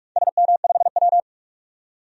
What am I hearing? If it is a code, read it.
Morse code, heard as a keyed tone, about 44 wpm, 700 Hz. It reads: SM5W
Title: DXCC-100 Call Signs - 4 Characters - 45wpm